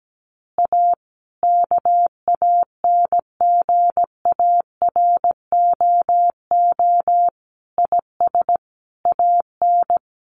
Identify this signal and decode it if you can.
Morse code, heard as a keyed tone, about 17 wpm, 700 Hz. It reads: A KANGAROO IS AN
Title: Easy Phrases 17wpm